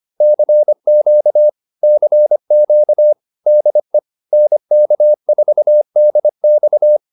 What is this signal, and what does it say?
Morse code, heard as a keyed tone, about 25 wpm, 600 Hz. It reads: CQ CQ DE NK4DX